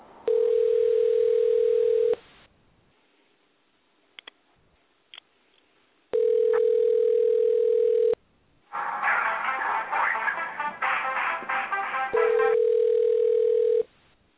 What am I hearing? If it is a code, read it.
Ringback tone (North America), 440 + 480 Hz.